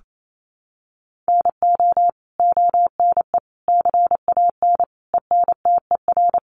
Morse code, heard as a keyed tone, about 28 wpm, 700 Hz. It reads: NO ONE CAN ENTER
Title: Sentences from Top 1300 Words 28wpm